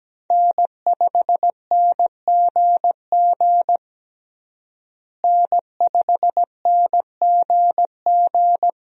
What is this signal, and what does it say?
Morse code, heard as a keyed tone, about 17 wpm, 700 Hz. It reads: N5NGG N5NGG
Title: USA Call Signs - 5 Characters - Repeated 3x - 17wpm